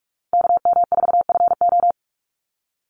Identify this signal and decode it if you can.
Morse code, heard as a keyed tone, about 44 wpm, 700 Hz. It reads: KM4FO
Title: Call Signs - 45wpm